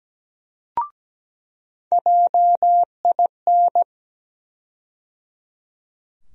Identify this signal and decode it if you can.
Morse code, heard as a keyed tone, about 17 wpm, 700 Hz. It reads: JIN
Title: USA Call Sign Postfixes - 3 Letters 17wpm